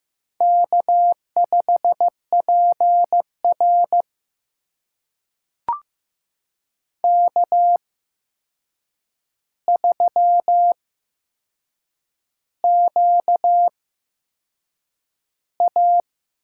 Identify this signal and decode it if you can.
Morse code, heard as a keyed tone, about 15 wpm, spaced as for 6 wpm, 700 Hz. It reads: K5PR K3QA
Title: USA Call Signs - 4 Characters - 8x Character Spacing - 15wpm